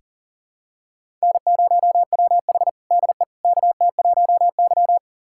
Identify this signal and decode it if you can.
Morse code, heard as a keyed tone, about 40 wpm, 700 Hz. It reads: N0WH DE KT1Y
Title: USA Callsign DE Callsign - 4 Characters - 40wpm